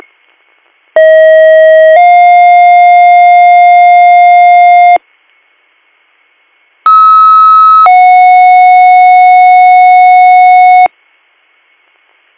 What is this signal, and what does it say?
Two-tone page, heard as a one-second tone, then a three-second tone, about 640 then 710 Hz, about 1290 then 730 Hz. It is heard from 1.0 to 10.9 s.